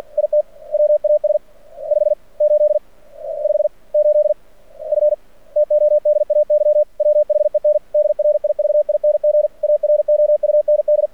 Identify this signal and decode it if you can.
Morse code, heard as a keyed tone, about 53 wpm, spaced as for 41 wpm, 600 Hz. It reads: IT WAS 58 58F TODAY WHEN DRIVING AROUND
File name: archived / CRSnet